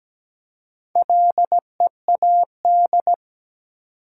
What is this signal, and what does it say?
Morse code, heard as a keyed tone, about 17 wpm, 700 Hz. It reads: LEAD
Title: Four Letter Words - 17wpm